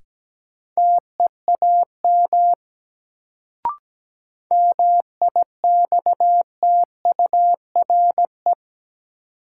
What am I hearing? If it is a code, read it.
Morse code, heard as a keyed tone, about 17 wpm, 700 Hz. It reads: TEAM MIXTURE
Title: Sentences from Top 1500 Words - New Word Drill - 17wpm